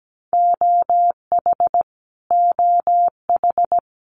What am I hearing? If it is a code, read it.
Morse code, heard as a keyed tone, about 17 wpm, 700 Hz. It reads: OH OH
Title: US States Repeated 3x 17wpm